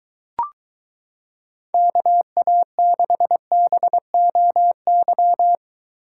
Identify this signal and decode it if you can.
Morse code, heard as a keyed tone, about 22 wpm, 700 Hz. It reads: KA6BOY